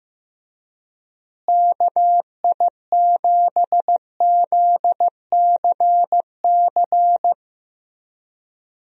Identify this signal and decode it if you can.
Morse code, heard as a keyed tone, about 15 wpm, 700 Hz. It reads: KI7ZCC
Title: USA Call Signs - 6 Characters - Repeated 2x - 15wpm